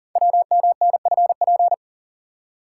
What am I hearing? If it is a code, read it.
Morse code, heard as a keyed tone, about 39 wpm, 700 Hz. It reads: WMNFP
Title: Code Groups - 5 Letters - 40wpm